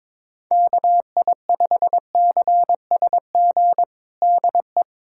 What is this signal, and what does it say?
Morse code, heard as a keyed tone, about 22 wpm, 700 Hz. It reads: KI5CSG DE
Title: USA Callsign DE Callsign - 6 Characters - 22wpm